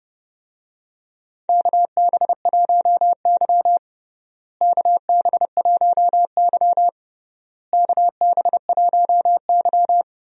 Morse code, heard as a keyed tone, about 29 wpm, 700 Hz. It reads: KB1Y KB1Y KB1Y
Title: USA Call Signs - 4 Characters - Repeated 3x - 30wpm